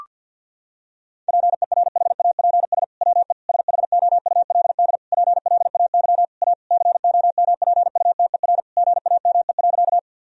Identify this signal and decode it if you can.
Morse code, heard as a keyed tone, about 50 wpm, 700 Hz. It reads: PERHAPS WE SHOULD PLAY A COMPUTER GAME.
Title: Sentences from Top 1000 Words - Review Part 3 - 50wpm